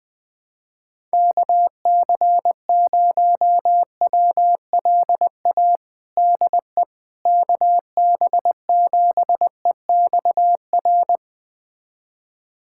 Morse code, heard as a keyed tone, about 20 wpm, 700 Hz. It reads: KC0WLA DE KB7EXR
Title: USA Callsign DE Callsign 20wpm